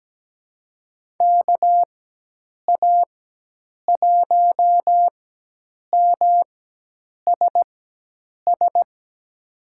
Morse code, heard as a keyed tone, about 17 wpm, spaced as for 8 wpm, 700 Hz. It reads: KA1MSS
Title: USA Call Signs - 6 Characters - 4x Character Spacing - 17wpm